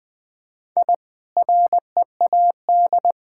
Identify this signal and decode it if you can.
Morse code, heard as a keyed tone, about 20 wpm, 700 Hz. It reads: I READ
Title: Easy Sentences with Hints - 20wpm